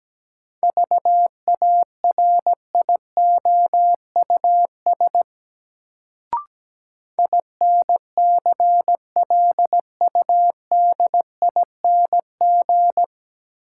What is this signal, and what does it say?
Morse code, heard as a keyed tone, about 17 wpm, 700 Hz. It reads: VARIOUS INCLUDING
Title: Sentences from Top 1000 Words - Review Part 4 - New Word Drill - 17wpm